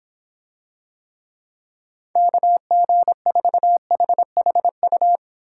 Morse code, heard as a keyed tone, about 26 wpm, 700 Hz. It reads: KG4HHU